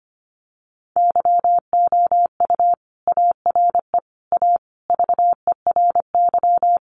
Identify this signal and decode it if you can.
Morse code, heard as a keyed tone, about 25 wpm, 700 Hz. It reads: YOU ARE A VERY
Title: Sentences from Top 1100 Words 25wpm